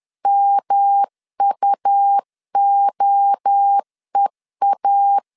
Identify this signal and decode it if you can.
Morse code, heard as a keyed tone, about 10 wpm, 800 Hz. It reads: MUOEA
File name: CW / DET / 1 3 1500C S B 10wpm 3335